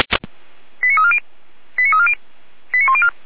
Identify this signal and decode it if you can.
Five-tone selective call (ZVEI), heard as quick runs of five tones, tones of about 70 ms. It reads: 89340 89340 89104